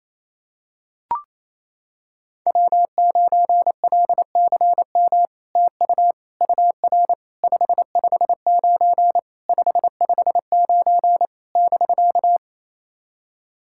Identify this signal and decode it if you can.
Morse code, heard as a keyed tone, about 28 wpm, 700 Hz. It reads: W9LCM TU UR 559 559 <BK>
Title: SOTA QSO Practice 28wpm